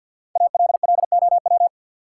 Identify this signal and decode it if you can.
Morse code, heard as a keyed tone, about 52 wpm, 700 Hz. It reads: ALLOW
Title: Sentences from Top 700 Words - New Word Drill - 50wpm